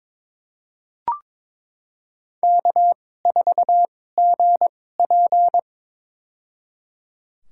Morse code, heard as a keyed tone, about 22 wpm, spaced as for 16 wpm, 700 Hz. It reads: K4GP
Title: USA Call Signs - 4 Characters - 2x Character Spacing - 22wpm